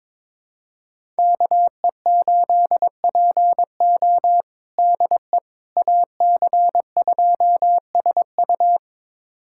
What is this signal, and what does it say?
Morse code, heard as a keyed tone, about 22 wpm, 700 Hz. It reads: KE8PO DE AC2SU
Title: USA Callsign DE Callsign - 5 Characters - 22wpm